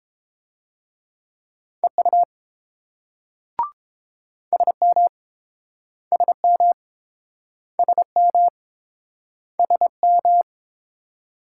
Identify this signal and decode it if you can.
Morse code, heard as a keyed tone, about 30 wpm, 700 Hz. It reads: EU SM SM SM SM